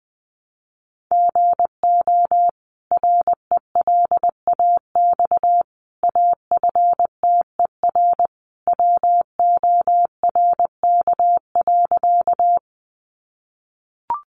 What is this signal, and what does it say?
Morse code, heard as a keyed tone, about 20 wpm, 700 Hz. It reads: GO RELAX AFTER WORK.
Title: Sentences from Top 1400 Words 20wpm